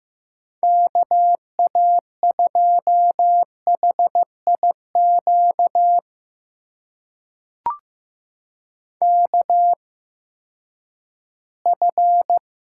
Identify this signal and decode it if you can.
Morse code, heard as a keyed tone, about 15 wpm, spaced as for 10 wpm, 700 Hz. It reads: KA2HIQ KF
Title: USA Call Signs - 8x Character Spacing - 15wpm